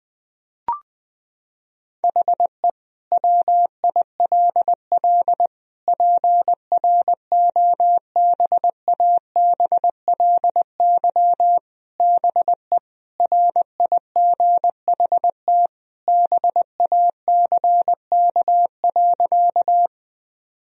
Morse code, heard as a keyed tone, about 21 wpm, 700 Hz. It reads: HE WILL PROBABLY BE RIGHT BACK.
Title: Sentences from Top 1000 Words - Review Part 4 - 20wpm